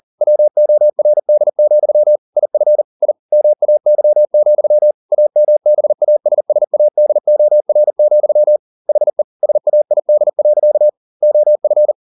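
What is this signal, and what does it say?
Morse code, heard as a keyed tone, about 40 wpm, 600 Hz. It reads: WORD, IF I MAY, AMBASSADOR, HE SAID. OF
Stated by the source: Morse code at 40 wpm, practice text, CW med Gnister.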